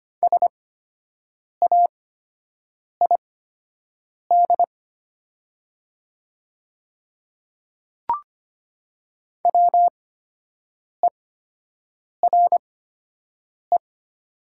Morse code, heard as a keyed tone, about 25 wpm, spaced as for 7 wpm, 700 Hz. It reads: SAID WERE
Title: 8x Character Spacing - Top 100 Words - 25wpm